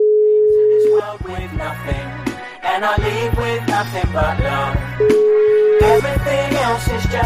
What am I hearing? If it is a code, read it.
Ringback tone (Europe, 425 Hz), with noise.